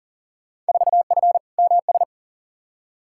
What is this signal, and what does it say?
Morse code, heard as a keyed tone, about 40 wpm, 700 Hz. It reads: 4F MS
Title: Field Day Exchanges Encore - 40wpm